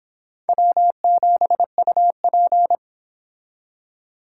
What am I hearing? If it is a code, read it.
Morse code, heard as a keyed tone, about 25 wpm, 700 Hz. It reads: W7UP